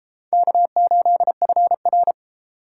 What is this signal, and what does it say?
Morse code, heard as a keyed tone, about 31 wpm, 700 Hz. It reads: K8FR